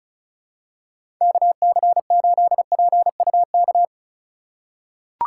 Morse code, heard as a keyed tone, about 36 wpm, 700 Hz. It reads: KC8PUK
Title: USA Call Signs - 6 Characters - 2x Character Spacing - 35wpm